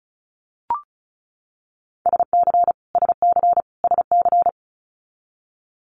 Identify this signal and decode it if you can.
Morse code, heard as a keyed tone, about 36 wpm, 700 Hz. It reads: SC SC SC
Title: Multi-letter Phonemes Repeated 3x 35wpm